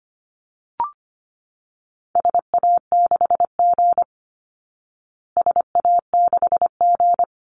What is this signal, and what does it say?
Morse code, heard as a keyed tone, about 25 wpm, 700 Hz. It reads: SA6G SA6G
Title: Call Signs Repeated 3x - 25wpm